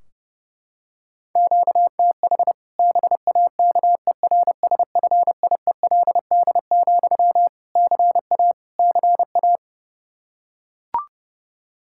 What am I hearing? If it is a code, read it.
Morse code, heard as a keyed tone, about 30 wpm, 700 Hz. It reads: QTH BAKERSFIELD, CA CA
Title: Practice QSO - Exchange 1 - Beginning - 30wpm